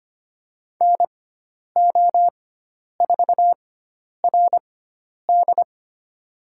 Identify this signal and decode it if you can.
Morse code, heard as a keyed tone, about 25 wpm, spaced as for 10 wpm, 700 Hz. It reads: NO4RD
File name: Lesson 16g - Letter c - Call Signs 25wpm